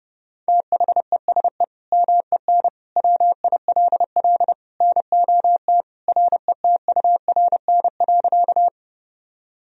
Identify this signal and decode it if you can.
Morse code, heard as a keyed tone, about 30 wpm, 700 Hz. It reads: THESE MEN WILL NOT RETURN.
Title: Sentences from Top 1000 Words - Review Part 1 - 30wpm